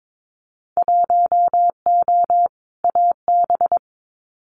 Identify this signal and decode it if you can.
Morse code, heard as a keyed tone, about 22 wpm, 700 Hz. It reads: 1O AB
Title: Winter Field Day Exchanges 22wpm